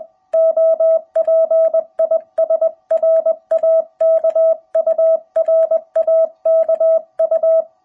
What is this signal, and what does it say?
Morse code, heard as a keyed tone, about 20 wpm, 660 Hz. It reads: OP IS RAKU RAKU
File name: jh8cbh / 2021cw